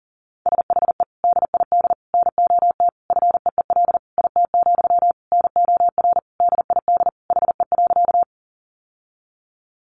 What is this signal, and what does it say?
Morse code, heard as a keyed tone, about 40 wpm, 700 Hz. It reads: SHE DID NOT FEEL IT, NOR DID HE.
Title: Sentences from Top 1500 Words 40wpm